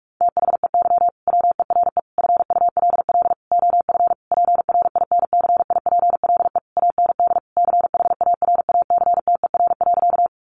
Morse code, heard as a keyed tone, about 44 wpm, 700 Hz. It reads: THEY WERE FULL OF PRINCIPLE AND CHARACTER.
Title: Sentences from Top 1400 Words 45wpm